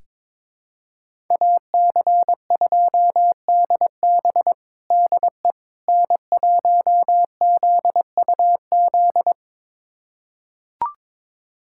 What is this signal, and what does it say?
Morse code, heard as a keyed tone, about 22 wpm, 700 Hz. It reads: AC2DB DE N1ZUZ
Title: USA Callsign DE Callsign - 5 Characters - 22wpm